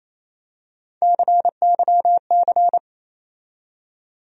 Morse code, heard as a keyed tone, about 28 wpm, 700 Hz. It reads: CYC
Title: USA Call Sign Postfixes - 3 Letters 28wpm